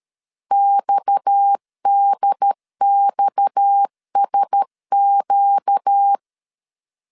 Morse code, heard as a keyed tone, about 13 wpm, 800 Hz. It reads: XDXSQ
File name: LEZIONI / 6 PKX0 / 6 PKX 12wpm 2024